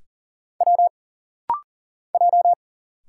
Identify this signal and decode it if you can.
Morse code, heard as a keyed tone, about 40 wpm, 700 Hz. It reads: W J
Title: Single Letters - Rapid-Fire 40wpm